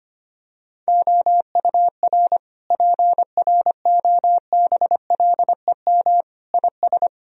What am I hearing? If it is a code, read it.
Morse code, heard as a keyed tone, about 25 wpm, 700 Hz. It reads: OUR PROBLEM IS
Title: Sentences from Top 1000 Words - Review Part 2 - 25wpm